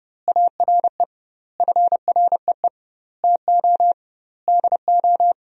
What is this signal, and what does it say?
Morse code, heard as a keyed tone, about 30 wpm, 700 Hz. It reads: ARE FREE TO DO
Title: 2x Word-Spacing - Sentences from Top 500 Words - 30wpm